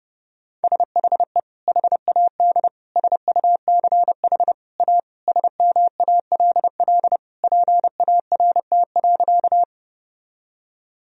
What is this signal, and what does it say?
Morse code, heard as a keyed tone, about 30 wpm, 700 Hz. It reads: SHE HAD SUCH A SMALL PART.
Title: Sentences from Top 1000 Words - Review Part 1 - 30wpm